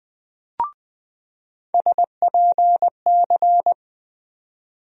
Morse code, heard as a keyed tone, about 20 wpm, 700 Hz. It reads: SPC
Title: USA Call Sign Postfixes - 3 Letters 20wpm